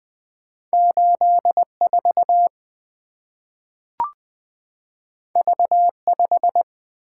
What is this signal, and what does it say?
Morse code, heard as a keyed tone, about 20 wpm, 700 Hz. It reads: 8 4 V 5
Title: Two Letter-Number Combinations - 20wpm